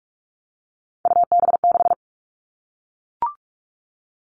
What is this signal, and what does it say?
Morse code, heard as a keyed tone, about 44 wpm, 700 Hz. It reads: UB6
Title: All Cut Numbers - 3-Digit 45wpm